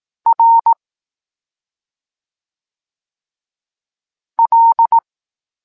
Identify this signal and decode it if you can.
Morse code, heard as a keyed tone, about 18 wpm, 920 Hz. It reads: R L
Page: Tekst 23.3 – Ciąg dalszy doskonalenia odbioru poznanych liter